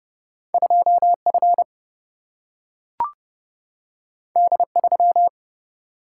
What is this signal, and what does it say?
Morse code, heard as a keyed tone, about 30 wpm, 700 Hz. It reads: 2F D3
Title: Single-Letter Single-Number 30wpm